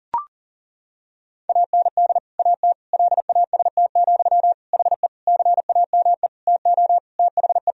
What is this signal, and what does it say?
Morse code, heard as a keyed tone, about 40 wpm, 700 Hz. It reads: AND AT LAST, HE CAME TO THE
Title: Sentences from Top 1000 Words - Review Part 2 - 40wpm